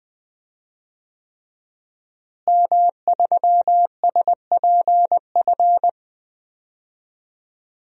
Morse code, heard as a keyed tone, about 20 wpm, 700 Hz. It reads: M3SPF
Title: DXCC-100 Call Signs - 5 Characters - Repeated 3x - 20wpm